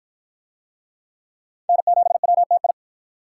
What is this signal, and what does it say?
Morse code, heard as a keyed tone, about 47 wpm, 700 Hz. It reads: N7WTI